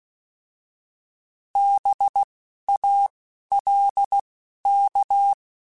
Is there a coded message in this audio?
Yes: Morse code from 1.6 to 5.3 s, heard as a keyed tone, about 16 wpm, 800 Hz. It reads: B A L K